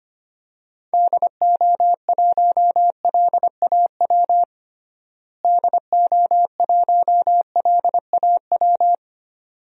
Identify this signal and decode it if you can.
Morse code, heard as a keyed tone, about 25 wpm, 700 Hz. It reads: DO1LAW DO1LAW
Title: DXCC-100 Call Signs - 6 Characters - Repeated 2x - 25wpm